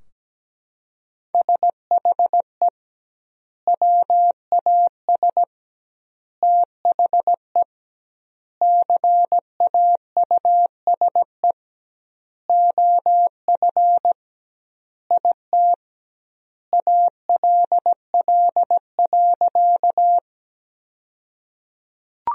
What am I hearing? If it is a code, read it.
Morse code, heard as a keyed tone, about 17 wpm, 700 Hz. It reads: SHE WAS THE CAUSE OF IT ALL.
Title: 2x Word-Spacing - Sentences from Top 200 Words - 17wpm